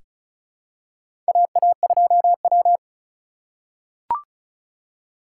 Morse code, heard as a keyed tone, about 36 wpm, 700 Hz. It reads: AA2W
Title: USA Call Signs - 4 Characters - 3x Character Spacing - 35wpm